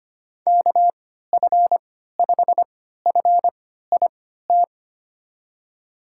Morse code, heard as a keyed tone, about 25 wpm, spaced as for 14 wpm, 700 Hz. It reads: KF5FIT
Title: USA Call Signs - 6 Characters - 3x Character Spacing - 25wpm